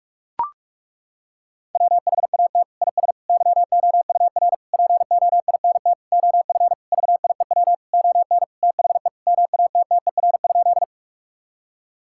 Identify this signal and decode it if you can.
Morse code, heard as a keyed tone, about 42 wpm, 700 Hz. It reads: WHAT IS YOUR POINT OF VIEW ON THE MATTER?
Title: Sentences from Top 1000 Words - Review Part 6 - 45wpm